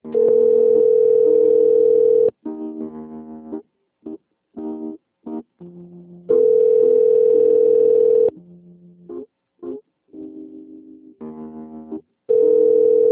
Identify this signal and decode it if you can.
Ringback tone (North America), 440 + 480 Hz.
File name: rg-1600-unknown- - - .37124